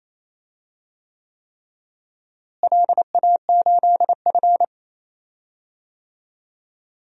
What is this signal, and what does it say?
Morse code, heard as a keyed tone, about 27 wpm, 700 Hz. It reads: LA8F
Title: DXCC-100 Call Signs - 4 Characters - Repeated 3x - 28wpm